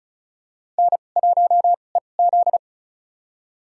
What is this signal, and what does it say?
Morse code, heard as a keyed tone, about 35 wpm, spaced as for 25 wpm, 700 Hz. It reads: N1EZ
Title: USA Call Signs - 4 Characters - 2x Character Spacing - 35wpm